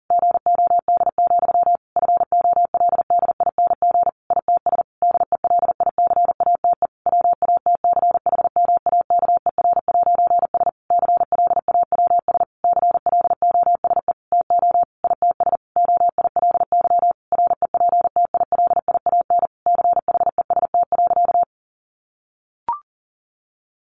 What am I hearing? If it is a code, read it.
Morse code, heard as a keyed tone, about 40 wpm, 700 Hz. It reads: GOD, FOLDING ITS DELICATE WATCHMAKER'S CLAWS CLOSE TO ITS OILY REPTILIAN CHEST.
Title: A Sound Of Thunder 40wpm